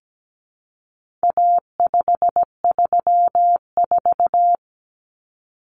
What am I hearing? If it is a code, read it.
Morse code, heard as a keyed tone, about 17 wpm, 700 Hz. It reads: A534
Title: Common Cut Numbers - 4-Digit 17wpm